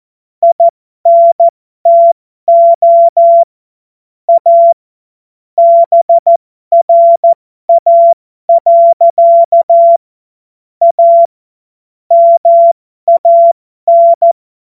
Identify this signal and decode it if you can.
Morse code, heard as a keyed tone, about 14 wpm, spaced as for 12 wpm, 680 Hz. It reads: INTO A BRA. A MAN
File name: Morsekurs / Witze